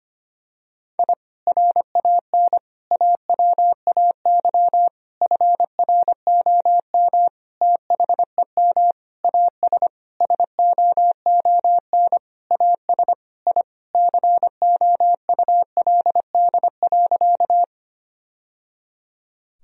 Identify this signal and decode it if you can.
Morse code, heard as a keyed tone, about 25 wpm, 700 Hz. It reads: I RAN AWAY FROM THEM AS SOON AS I COULD.
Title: Sentences from Top 1000 Words - Review Part 3 - 25wpm